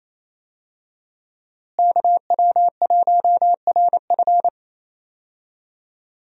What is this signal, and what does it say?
Morse code, heard as a keyed tone, about 28 wpm, 700 Hz. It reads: KW1RF
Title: USA Call Signs - 5 Characters - Repeated 2x - 28wpm